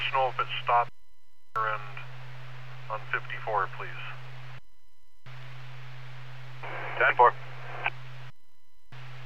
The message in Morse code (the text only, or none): none